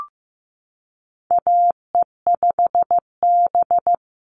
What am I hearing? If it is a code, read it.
Morse code, heard as a keyed tone, about 15 wpm, 700 Hz. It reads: AE5B
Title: NA SOTA Chaser Callsigns 15wpm